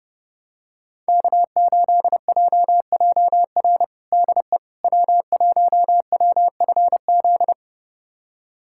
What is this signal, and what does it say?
Morse code, heard as a keyed tone, about 30 wpm, 700 Hz. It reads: K8JJR DE W1WFZ
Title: USA Callsign DE Callsign - 5 Characters - 30wpm